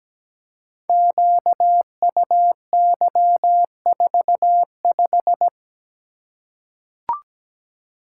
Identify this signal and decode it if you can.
Morse code, heard as a keyed tone, about 17 wpm, 700 Hz. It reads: QUY45
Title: Code Groups - 5 Letter:Numbers - 17wpm